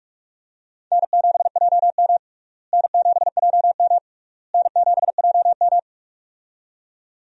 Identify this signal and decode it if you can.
Morse code, heard as a keyed tone, about 46 wpm, 700 Hz. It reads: N7JM N7JM N7JM
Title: USA Call Signs - 4 Characters - Repeated 3x - 45wpm